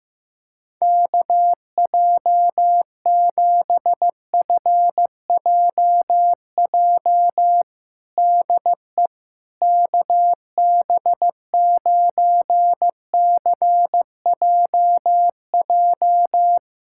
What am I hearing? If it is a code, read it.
Morse code, heard as a keyed tone, about 15 wpm, 700 Hz. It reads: KJ7FJJ DE KB9CJJ
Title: USA Callsign DE Callsign - 6 Characters - 15wpm